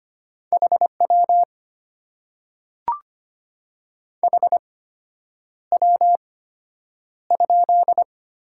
Morse code, heard as a keyed tone, about 25 wpm, spaced as for 9 wpm, 700 Hz. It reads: HW HW?
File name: Top 90 QSO Elements - 8x Character Spacing - 25wpm